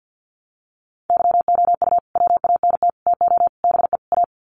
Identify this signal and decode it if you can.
Morse code, heard as a keyed tone, about 48 wpm, 700 Hz. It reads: YOU WANT TO BE A
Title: Sets of 5 Words Encore2 50wpm